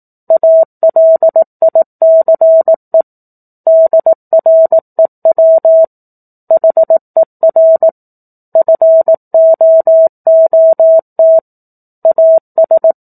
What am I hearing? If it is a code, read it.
Morse code, heard as a keyed tone, about 18 wpm, 650 Hz. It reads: ALICE DREW HER FOOT AS
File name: 04-Chapter-04 18 WPM